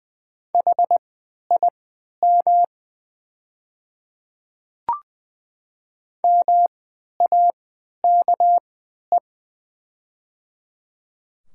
Morse code, heard as a keyed tone, about 20 wpm, spaced as for 11 wpm, 700 Hz. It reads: HIM MAKE
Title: 3x Character Spacing - Top 100 Words - 20wpm